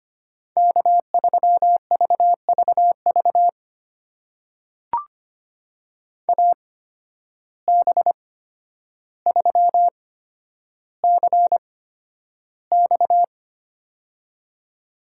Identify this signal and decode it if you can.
Morse code, heard as a keyed tone, about 25 wpm, spaced as for 11 wpm, 700 Hz. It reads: K3VVV AB3CX
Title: USA Call Signs - 5 Characters - 8x Character Spacing - 25wpm